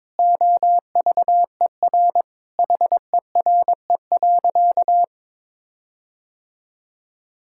Morse code, heard as a keyed tone, about 22 wpm, 700 Hz. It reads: OVER HERE.
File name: practice / 22wpm / Sentences with 4-Word Phrases - 22wpm